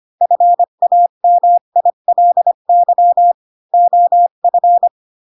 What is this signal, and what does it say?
Morse code, heard as a keyed tone, about 25 wpm, 700 Hz. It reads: FAMILY OF
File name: Pinocchio / Pinocchio ch3 20 25 ik1hge